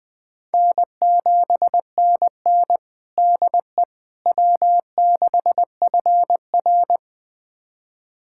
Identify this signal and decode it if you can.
Morse code, heard as a keyed tone, about 20 wpm, 700 Hz. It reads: N7NN DE W6FR
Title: USA Callsign DE Callsign - 4 Characters - 20wpm